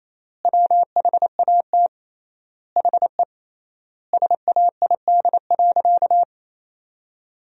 Morse code, heard as a keyed tone, about 28 wpm, 700 Hz. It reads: WHAT HE SAID.
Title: 3x Word-Spacing - Sentences from Top 400 Words - 28wpm